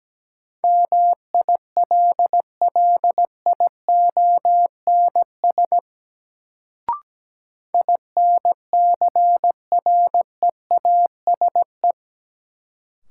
Morse code, heard as a keyed tone, about 17 wpm, 700 Hz. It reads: MILLIONS INCREASE
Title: Sets of 5 Words - New Word Drill - 17wpm